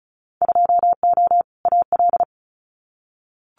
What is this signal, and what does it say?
Morse code, heard as a keyed tone, about 36 wpm, 690 Hz. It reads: 2O AL
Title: Winter Field Day Exchanges 35wpm